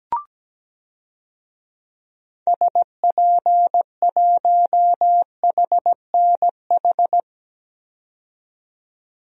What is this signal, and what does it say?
Morse code, heard as a keyed tone, about 17 wpm, 700 Hz. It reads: SP1HNH
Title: DXCC-100 Call Signs - 6 Characters - 17wpm